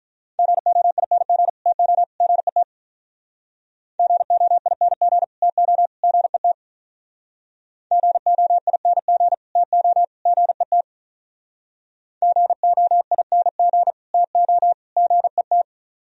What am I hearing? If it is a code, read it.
Morse code, heard as a keyed tone, about 45 wpm, 700 Hz. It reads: GOING TO GET GOING TO GET GOING TO GET GOING TO GET